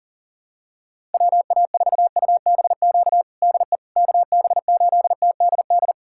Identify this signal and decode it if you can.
Morse code, heard as a keyed tone, about 40 wpm, 700 Hz. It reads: WA4UBQ DE KB8TDD
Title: USA Callsign DE Callsign - 6 Characters - 40wpm